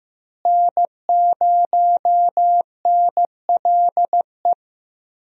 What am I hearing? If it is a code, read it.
Morse code, heard as a keyed tone, about 15 wpm, 700 Hz. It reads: N0NLE
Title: USA Call Signs - 5 Characters - Repeated 3x - 15wpm